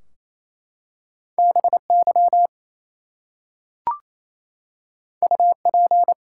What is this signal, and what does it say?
Morse code, heard as a keyed tone, about 27 wpm, 700 Hz. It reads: BY UP
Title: Two Letter Words - 28wpm